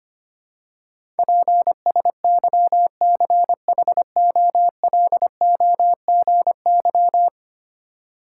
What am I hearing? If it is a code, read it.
Morse code, heard as a keyed tone, about 25 wpm, 700 Hz. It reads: PSYCHOLOGY
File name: 10 Letter Words - First 3 Letters Given - 25wpm